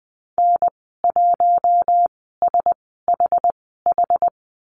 Morse code, heard as a keyed tone, about 20 wpm, spaced as for 14 wpm, 700 Hz. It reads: N1SHH
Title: 082 Lesson 14g - Number 2 - Call Signs 20wpm